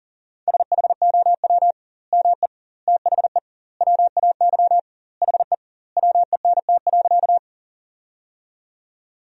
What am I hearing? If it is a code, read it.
Morse code, heard as a keyed tone, about 40 wpm, 700 Hz. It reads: SHOW ME THE WAY HE WENT.
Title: 2x Word-Spacing - Sentences from Top 200 Words - 40wpm